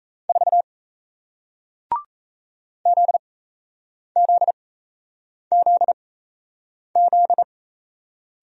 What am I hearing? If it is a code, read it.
Morse code, heard as a keyed tone, about 38 wpm, spaced as for 8 wpm, 700 Hz. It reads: 4 ZZZZ